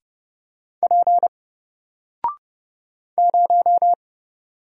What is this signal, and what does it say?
Morse code, heard as a keyed tone, about 30 wpm, 700 Hz. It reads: P 0
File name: Single-Letter-Number-30wpm